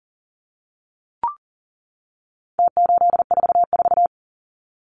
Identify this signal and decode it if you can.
Morse code, heard as a keyed tone, about 40 wpm, 700 Hz. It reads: T844
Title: Numbers - Contest Style - 4-Digit 40wpm